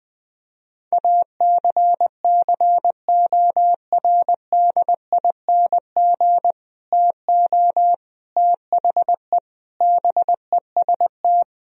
Morse code, heard as a keyed tone, about 20 wpm, 700 Hz. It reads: ACCORDING TO THE BEST
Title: Sentences from Top 1000 Words - Review Part 4 - 20wpm